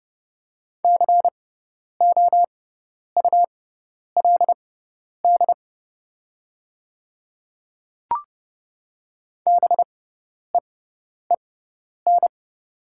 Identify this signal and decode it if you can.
Morse code, heard as a keyed tone, about 30 wpm, spaced as for 10 wpm, 700 Hz. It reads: COULD BEEN
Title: 6x Character Spacing - Top 100 Words - 30wpm